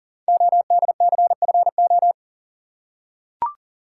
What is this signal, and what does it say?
Morse code, heard as a keyed tone, about 43 wpm, 700 Hz. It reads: ODCFO
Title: Code Groups - 5 Letters - 40wpm